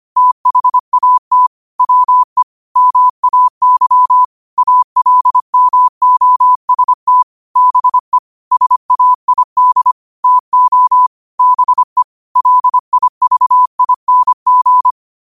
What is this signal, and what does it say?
Morse code, heard as a keyed tone, about 25 wpm, 1000 Hz. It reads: THAT WE MAY ALMOST BE SAID TO BE LIVING